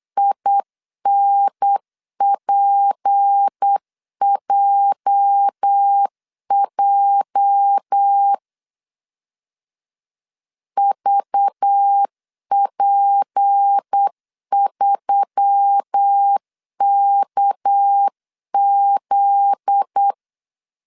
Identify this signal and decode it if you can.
Morse code, heard as a keyed tone, about 8 wpm, 800 Hz. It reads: INPJJ VP3KZ
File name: CW / DET / 1 9 1500C S A 8wpm 5637